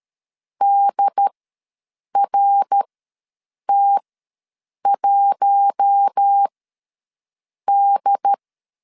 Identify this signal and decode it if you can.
Morse code, heard as a keyed tone, about 13 wpm, spaced as for 7 wpm, 800 Hz. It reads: DRT1D
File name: LEZIONI / 5 LBVC / 5 LBV ws 12wpm 3015